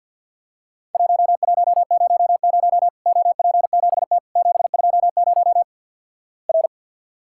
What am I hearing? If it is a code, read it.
Morse code, heard as a keyed tone, about 53 wpm, 700 Hz. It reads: 1100 OP7T 720 R
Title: Worked All Europe DX Contest - QTC Exchange with Repeats - 50wpm